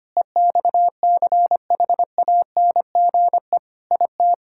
Morse code, heard as a keyed tone, about 25 wpm, 700 Hz. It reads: EXCHANGE IT
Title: Sentences from Top 1000 Words - Review Part 6 - 25wpm